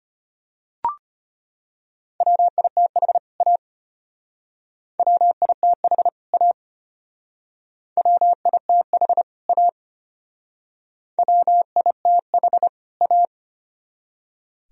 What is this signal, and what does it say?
Morse code, heard as a keyed tone, about 30 wpm, 700 Hz. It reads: WITH A WITH A WITH A WITH A
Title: Speed-Racing - Sets of 2 Words 25wpm